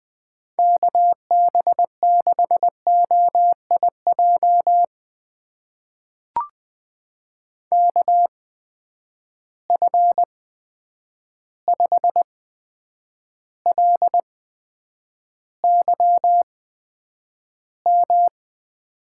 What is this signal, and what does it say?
Morse code, heard as a keyed tone, about 20 wpm, spaced as for 9 wpm, 700 Hz. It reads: KB6OIJ KF5LYM
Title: USA Call Signs - 6 Characters - 8x Character Spacing - 20wpm